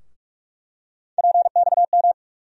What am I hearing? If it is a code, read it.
Morse code, heard as a keyed tone, about 46 wpm, 700 Hz. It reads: PXM
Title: USA Call Sign Postfixes - 3 Letters 45wpm